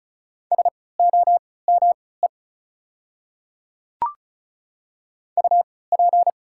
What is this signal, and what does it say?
Morse code, heard as a keyed tone, about 36 wpm, spaced as for 20 wpm, 700 Hz. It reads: SOME UP
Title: 3x Character Spacing - Top 100 Words - 35wpm